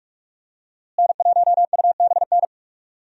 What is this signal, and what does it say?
Morse code, heard as a keyed tone, about 46 wpm, 700 Hz. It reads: N1UBN
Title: USA Call Signs - 5 Characters - Repeated 2x - 45wpm